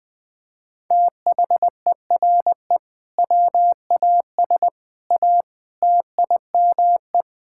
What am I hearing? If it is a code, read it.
Morse code, heard as a keyed tone, about 20 wpm, 700 Hz. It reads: THERE WAS A TIME
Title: Sets of 8 Words - 20wpm